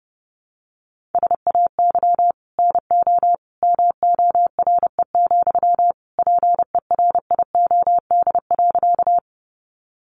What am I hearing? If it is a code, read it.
Morse code, heard as a keyed tone, about 30 wpm, 700 Hz. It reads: SAY NO MORE, PERIOD.
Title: Sentences from Top 800 Words 30wpm